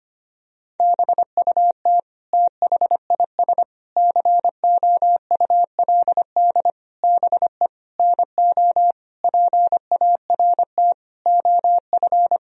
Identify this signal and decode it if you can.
Morse code, heard as a keyed tone, about 25 wpm, 700 Hz. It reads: BUT THIS COULD BE NO PART OF
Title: Sentences from Top 1000 Words - Review Part 2 - 25wpm